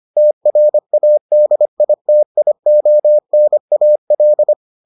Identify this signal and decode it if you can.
Morse code, heard as a keyed tone, about 25 wpm, 600 Hz. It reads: TRADITIONAL